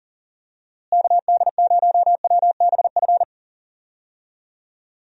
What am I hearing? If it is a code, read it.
Morse code, heard as a keyed tone, about 40 wpm, 700 Hz. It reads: KD0WBF
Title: USA Call Signs - 6 Characters - Repeated 3x - 40wpm